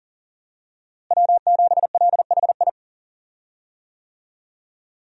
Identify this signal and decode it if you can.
Morse code, heard as a keyed tone, about 40 wpm, 700 Hz. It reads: W7LHI
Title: USA Call Signs - 5 Characters - Repeated 3x - 40wpm